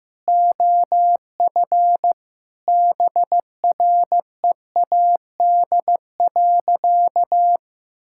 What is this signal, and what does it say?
Morse code, heard as a keyed tone, about 15 wpm, 700 Hz. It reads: OF BREAD.
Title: Sentences from Top 1600 Words 15wpm